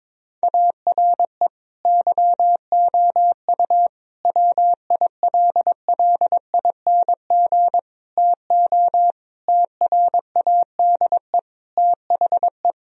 Morse code, heard as a keyed tone, about 22 wpm, 700 Hz. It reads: ARE YOU WILLING TO TRADE THE
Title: Sentences from Top 1000 Words - Review Part 4 - 22wpm